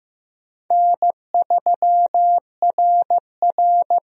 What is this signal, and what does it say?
Morse code, heard as a keyed tone, about 16 wpm, 700 Hz. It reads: N3RR
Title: USA Call Signs - 4 Characters - Repeated 3x - 15wpm